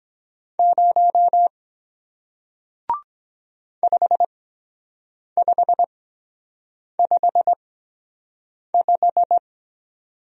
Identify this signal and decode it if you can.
Morse code, heard as a keyed tone, about 23 wpm, spaced as for 7 wpm, 700 Hz. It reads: 0 5555